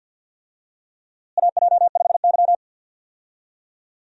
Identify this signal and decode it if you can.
Morse code, heard as a keyed tone, about 51 wpm, 700 Hz. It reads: AJ5Y
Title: USA Call Signs - 4 Characters - Repeated 3x - 50wpm